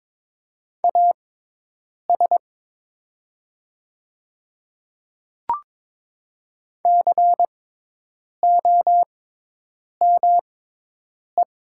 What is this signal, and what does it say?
Morse code, heard as a keyed tone, about 22 wpm, spaced as for 8 wpm, 700 Hz. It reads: AS COME
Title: 6x Character Spacing - Top 100 Words - 22wpm